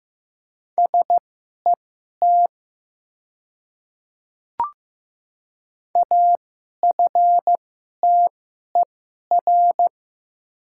Morse code, heard as a keyed tone, about 15 wpm, spaced as for 11 wpm, 700 Hz. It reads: SET AFTER
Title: 2x Character Spacing - Top 300 Words - 15wpm